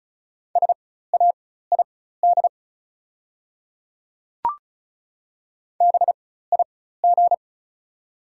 Morse code, heard as a keyed tone, about 35 wpm, spaced as for 16 wpm, 700 Hz. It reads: SAID BIG
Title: 4x Character Spacing - Top 200 Words - 35wpm